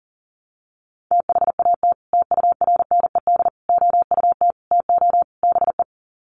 Morse code, heard as a keyed tone, about 40 wpm, 700 Hz. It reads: THAT TURNED OUT TO BE
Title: Sets of 5 Words Encore2 40wpm